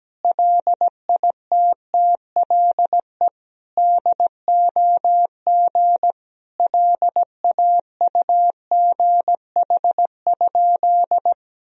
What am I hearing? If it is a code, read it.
Morse code, heard as a keyed tone, about 17 wpm, 700 Hz. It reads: LITTLE DOG LAUGH?
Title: Sentences from Top 500 Words and Abbreviations 17wpm